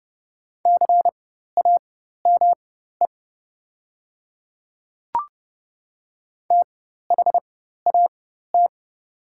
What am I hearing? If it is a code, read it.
Morse code, heard as a keyed tone, about 30 wpm, spaced as for 14 wpm, 700 Hz. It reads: CAME THAT
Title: 4x Character Spacing - Top 300 Words - 30wpm